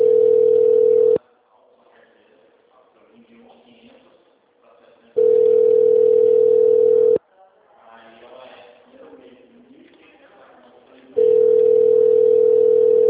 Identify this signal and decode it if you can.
Ringback tone (North America), 440 + 480 Hz, with speech.
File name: out- -2008- - - .10751